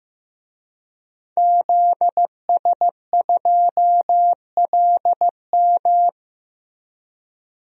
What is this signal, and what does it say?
Morse code, heard as a keyed tone, about 15 wpm, 700 Hz. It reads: ZS2LM
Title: DXCC-100 Call Signs - 5 Characters - Repeated 3x - 15wpm